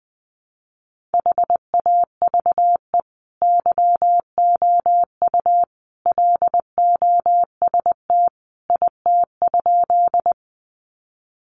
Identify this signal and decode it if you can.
Morse code, heard as a keyed tone, about 20 wpm, 700 Hz. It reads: HAVE YOU LOST IT?
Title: Sentences from Top 1400 Words 20wpm